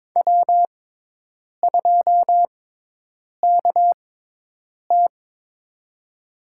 Morse code, heard as a keyed tone, about 22 wpm, spaced as for 8 wpm, 700 Hz. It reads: W2KT
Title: USA Call Signs - 4 Characters - 6x Character Spacing - 22wpm